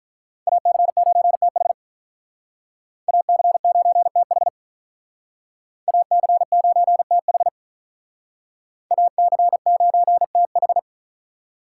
Morse code, heard as a keyed tone, about 46 wpm, 700 Hz. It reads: AC9TH AC9TH AC9TH AC9TH